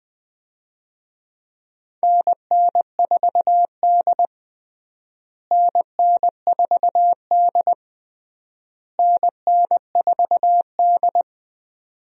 Morse code, heard as a keyed tone, about 21 wpm, 700 Hz. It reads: NN4D NN4D NN4D
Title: USA Call Signs - 4 Characters - Repeated 3x - 20wpm